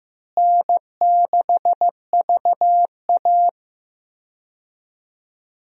Morse code, heard as a keyed tone, about 15 wpm, 700 Hz. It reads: N6VA
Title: USA Call Signs - 4 Characters - Repeated 3x - 15wpm